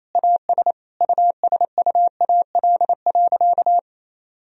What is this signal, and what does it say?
Morse code, heard as a keyed tone, about 28 wpm, 700 Hz. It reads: AS USUAL.
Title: Sentences from Top 1000 Words - Review Part 2 - 28wpm